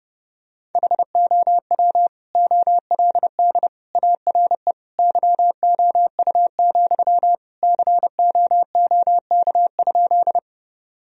Morse code, heard as a keyed tone, about 32 wpm, 700 Hz. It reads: HOW OLD ARE YOU, COOK?
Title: Sentences from Top 1000 Words - Review Part 4 - 30wpm